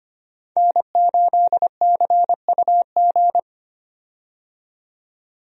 Morse code, heard as a keyed tone, about 25 wpm, 700 Hz. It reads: N8CUG
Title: USA Call Signs - 5 Characters - 25wpm